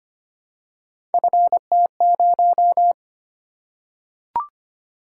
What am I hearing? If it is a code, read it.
Morse code, heard as a keyed tone, about 25 wpm, 700 Hz. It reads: FT0
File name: Three Letter-Number Combinations - 25wpm